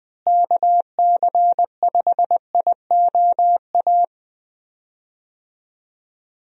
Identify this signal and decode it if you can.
Morse code, heard as a keyed tone, about 21 wpm, 700 Hz. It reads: KC5IOA
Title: USA Call Signs - 6 Characters - Repeated 3x - 20wpm